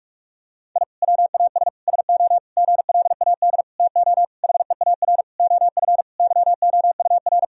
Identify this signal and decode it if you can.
Morse code, heard as a keyed tone, about 43 wpm, 700 Hz. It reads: I WAS SO GLAD TO HEAR OF YOUR
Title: Sentences from Top 1000 Words - Review Part 4 - 45wpm